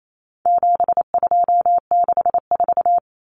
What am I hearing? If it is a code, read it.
Morse code, heard as a keyed tone, about 28 wpm, 700 Hz. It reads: 7264
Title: Numbers - 4-Digit 28wpm